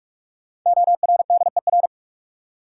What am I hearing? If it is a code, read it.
Morse code, heard as a keyed tone, about 46 wpm, 700 Hz. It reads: ORDER
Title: Sets of 2 Words - New Word Drill - 45wpm